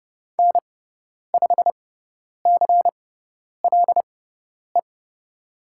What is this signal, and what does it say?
Morse code, heard as a keyed tone, about 30 wpm, spaced as for 10 wpm, 700 Hz. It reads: N5CLE
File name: Lesson 17g - Letter u - Call Signs 30wpm